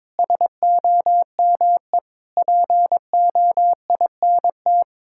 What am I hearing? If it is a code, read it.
Morse code, heard as a keyed tone, about 22 wpm, 700 Hz. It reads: SOME POINT
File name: Sentences with 7-Word Phrases - 22wpm